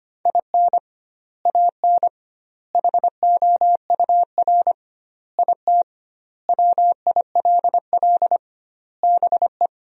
Morse code, heard as a keyed tone, about 25 wpm, 700 Hz. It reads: IN AN HOUR IT WILL BE
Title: 2x Word-Spacing - Sentences from Top 400 Words - 25wpm